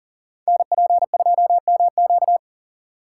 Morse code, heard as a keyed tone, about 39 wpm, 700 Hz. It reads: NP2MQ
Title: Code Groups - 5 Letter:Numbers - 40wpm